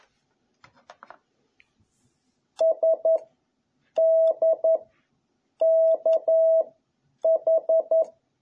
Morse code, heard as a keyed tone, about 11 wpm, 660 Hz. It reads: S D K H
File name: jh8cbh / 2021cw